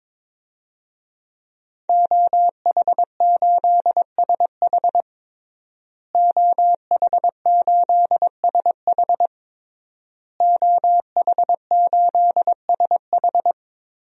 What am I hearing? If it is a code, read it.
Morse code, heard as a keyed tone, about 22 wpm, 700 Hz. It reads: OH8SH OH8SH OH8SH
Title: DXCC-100 Call Signs - 5 Characters - Repeated 3x - 22wpm